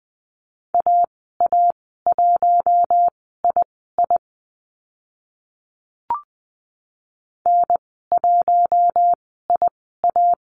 Morse code, heard as a keyed tone, about 20 wpm, spaced as for 14 wpm, 700 Hz. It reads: AA1II N1IA
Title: 062 Lesson 11g - Letter h - Call Signs 20wpm